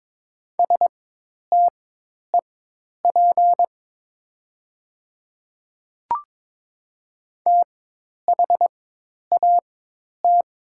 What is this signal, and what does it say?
Morse code, heard as a keyed tone, about 22 wpm, spaced as for 10 wpm, 700 Hz. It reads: STEP THAT
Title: 4x Character Spacing - Top 500 Words - 22wpm